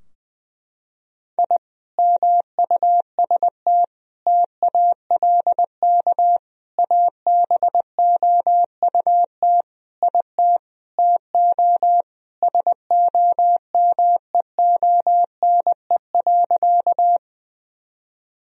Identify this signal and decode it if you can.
Morse code, heard as a keyed tone, about 20 wpm, 700 Hz. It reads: I MUST TALK ABOUT IT TO SOMEONE.
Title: Sentences from Top 1000 Words - Review Part 4 - 20wpm